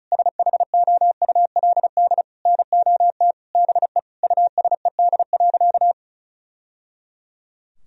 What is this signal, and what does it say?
Morse code, heard as a keyed tone, about 34 wpm, 700 Hz. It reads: SHOULD NOT BE USED.
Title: Sentences from Top 1000 Words - Review Part 4 - 35wpm